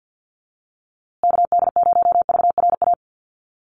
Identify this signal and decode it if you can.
Morse code, heard as a keyed tone, about 50 wpm, 700 Hz. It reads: KD0VRA
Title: USA Call Signs Repeated 2x 50wpm